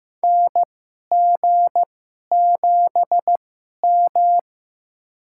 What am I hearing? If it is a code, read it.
Morse code, heard as a keyed tone, about 15 wpm, spaced as for 11 wpm, 700 Hz. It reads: NG7M
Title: USA Call Signs - 2x Character Spacing - 15wpm